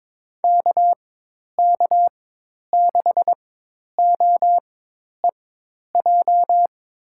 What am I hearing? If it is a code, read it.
Morse code, heard as a keyed tone, about 21 wpm, spaced as for 10 wpm, 700 Hz. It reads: KK6OEJ
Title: USA Call Signs - 6 Characters - 4x Character Spacing - 22wpm